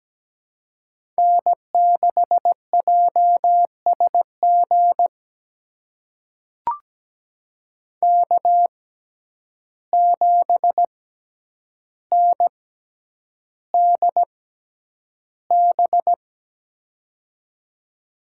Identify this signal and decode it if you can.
Morse code, heard as a keyed tone, about 17 wpm, spaced as for 9 wpm, 700 Hz. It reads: N6JSG K7NDB
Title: USA Call Signs - 5 Characters - 6x Character Spacing - 17wpm